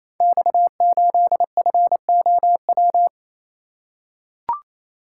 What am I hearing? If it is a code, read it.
Morse code, heard as a keyed tone, about 27 wpm, 700 Hz. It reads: X8FOW
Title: Code Groups - 5 Letter:Numbers - 28wpm